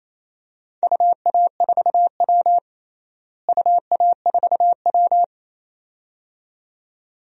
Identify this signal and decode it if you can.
Morse code, heard as a keyed tone, about 28 wpm, 700 Hz. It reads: UA4W UA4W
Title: DXCC-100 Call Signs - 4 Characters - Repeated 2x - 28wpm